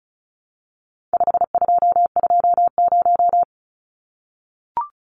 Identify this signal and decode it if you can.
Morse code, heard as a keyed tone, about 34 wpm, 700 Hz. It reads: 5220
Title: Numbers - 4-Digit 35wpm